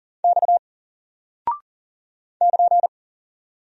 Morse code, heard as a keyed tone, about 40 wpm, 700 Hz. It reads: X <KN>
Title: Single Character (Letter:Number:Punctuation:Prosign) Alt 40wpm